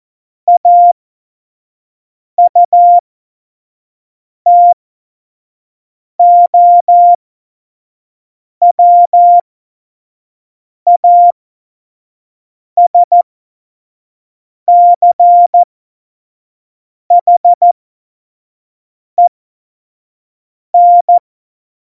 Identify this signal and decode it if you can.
Morse code, heard as a keyed tone, about 14 wpm, spaced as for 5 wpm, 700 Hz. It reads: AUTOWASCHEN
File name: Morsekurs / Witze / Witz 14 Juli 2020, 25 F70